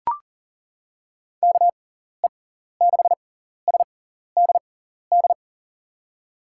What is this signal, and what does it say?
Morse code, heard as a keyed tone, about 40 wpm, spaced as for 14 wpm, 700 Hz. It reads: KE6SDD
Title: USA Call Signs - 6 Characters - 6x Character Spacing - 40wpm